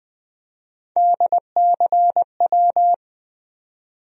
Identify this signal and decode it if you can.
Morse code, heard as a keyed tone, about 21 wpm, 700 Hz. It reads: DCW
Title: USA Call Sign Postfixes - 3 Letters 20wpm